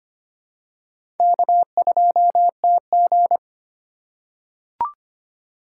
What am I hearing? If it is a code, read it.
Morse code, heard as a keyed tone, about 25 wpm, 700 Hz. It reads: K2TG
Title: USA Call Signs - 4 Characters - 3x Character Spacing - 25wpm